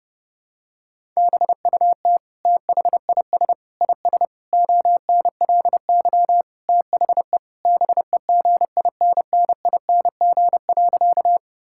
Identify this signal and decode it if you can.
Morse code, heard as a keyed tone, about 30 wpm, 700 Hz. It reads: BUT THIS IS ONLY THE BEGINNING.
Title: Sentences from Top 1000 Words - Review Part 5 - 30wpm